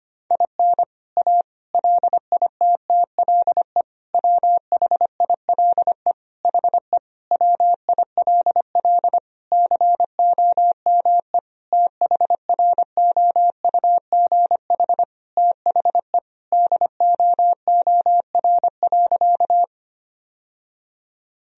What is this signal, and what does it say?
Morse code, heard as a keyed tone, about 25 wpm, 700 Hz. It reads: IN A LITTLE WHILE HE WILL COME THROUGH THE DOOR.
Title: Sentences from Top 1000 Words - Review Part 1 - 25wpm